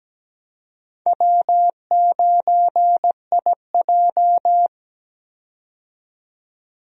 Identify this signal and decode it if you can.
Morse code, heard as a keyed tone, about 17 wpm, 700 Hz. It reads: W9IJ
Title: USA Call Signs - 4 Characters - 17wpm